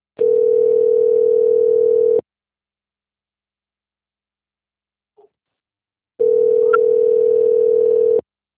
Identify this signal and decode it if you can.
Ringback tone (North America), 440 + 480 Hz.
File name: rg-1600-unknown- - - .13475